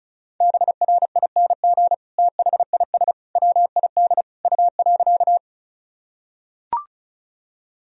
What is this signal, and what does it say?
Morse code, heard as a keyed tone, about 34 wpm, 700 Hz. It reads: BRING THIS WID U.
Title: Sentences from Top 500 Words and Abbreviations 35wpm